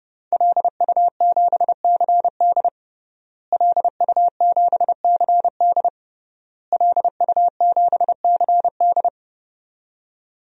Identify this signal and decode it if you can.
Morse code, heard as a keyed tone, about 30 wpm, 700 Hz. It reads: LU7CD LU7CD LU7CD
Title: DXCC-100 Call Signs - 5 Characters - Repeated 3x - 30wpm